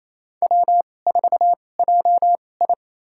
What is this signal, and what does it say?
Morse code, heard as a keyed tone, about 28 wpm, spaced as for 20 wpm, 700 Hz. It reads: W4JI
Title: USA Call Signs - 2x Character Spacing - 28wpm